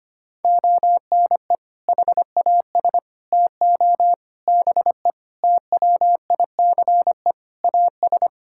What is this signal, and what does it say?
Morse code, heard as a keyed tone, about 25 wpm, 700 Hz. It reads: ONE HAS TO BE TWICE AS
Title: Sentences from Top 1600 Words 25wpm